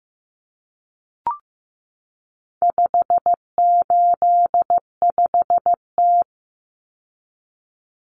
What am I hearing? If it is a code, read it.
Morse code, heard as a keyed tone, about 15 wpm, 700 Hz. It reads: 585T
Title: Common Cut Numbers - 4-Digit 15wpm